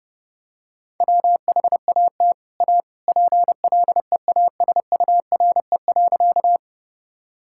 Morse code, heard as a keyed tone, about 32 wpm, 700 Hz. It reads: WHAT A PLEASURE.
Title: Sentences from Top 1600 Words 30wpm